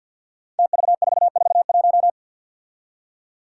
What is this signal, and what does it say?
Morse code, heard as a keyed tone, about 52 wpm, 700 Hz. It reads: TV441
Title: Five Challenging Letter-Number Combinations 50wpm